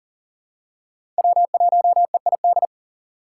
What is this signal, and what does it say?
Morse code, heard as a keyed tone, about 37 wpm, 700 Hz. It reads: W1EID
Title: USA Call Signs - 5 Characters - 3x Character Spacing - 40wpm